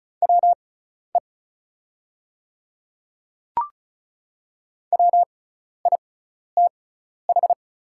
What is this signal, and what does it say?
Morse code, heard as a keyed tone, about 36 wpm, spaced as for 12 wpm, 700 Hz. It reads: WE WITH
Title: 6x Character Spacing - Top 100 Words - 35wpm